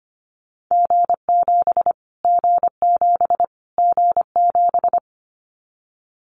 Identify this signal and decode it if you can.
Morse code, heard as a keyed tone, about 25 wpm, 700 Hz. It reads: G7 G7 G7
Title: Callsign Prefixes for Top 25 DXCC - Repeated 3x - 25wpm